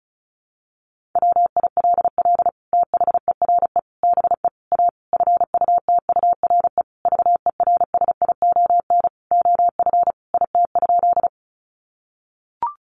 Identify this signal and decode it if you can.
Morse code, heard as a keyed tone, about 34 wpm, 700 Hz. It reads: WILL THERE BE A FUTURE VERSION OF IT?
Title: Sentences from Top 1400 Words 35wpm